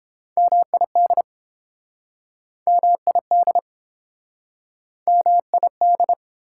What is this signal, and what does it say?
Morse code, heard as a keyed tone, about 32 wpm, 700 Hz. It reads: MID MID MID